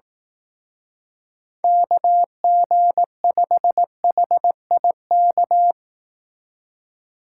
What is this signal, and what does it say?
Morse code, heard as a keyed tone, about 18 wpm, 700 Hz. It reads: KG5HIK
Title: Speed Racing - USA Call Signs 15wpm